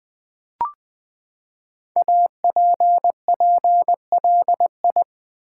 Morse code, heard as a keyed tone, about 20 wpm, 700 Hz. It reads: APPLI
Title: Build A Word - 5 Letters At A Time - 10-Letter Words - 20wpm